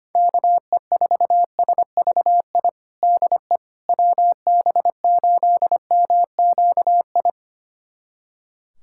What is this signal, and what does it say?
Morse code, heard as a keyed tone, about 25 wpm, 700 Hz. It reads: KE4SVI DE WB8MQI
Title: USA Callsign DE Callsign - 6 Characters - 25wpm